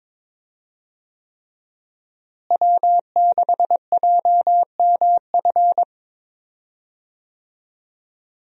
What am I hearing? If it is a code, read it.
Morse code, heard as a keyed tone, about 22 wpm, 700 Hz. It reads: W6JMF
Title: USA Call Signs - 5 Characters - Repeated 3x - 22wpm